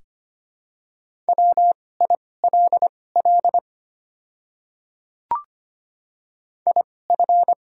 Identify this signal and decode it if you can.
Morse code, heard as a keyed tone, about 25 wpm, spaced as for 18 wpm, 700 Hz. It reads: WILL IF
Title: 2x Character Spacing - Top 200 Words - 25wpm